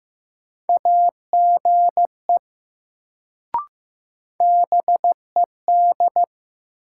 Morse code, heard as a keyed tone, about 15 wpm, 700 Hz. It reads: AGE BED
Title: Sentences from Top 500 Words - New Word Drill - 15wpm